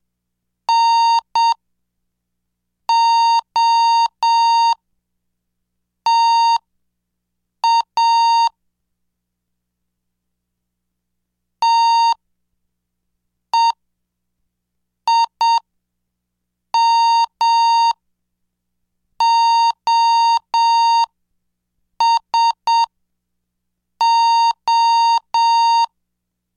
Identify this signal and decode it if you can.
Morse code, heard as a keyed tone, about 7 wpm, spaced as for 5 wpm, 920 Hz. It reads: NOTA TEIMOSO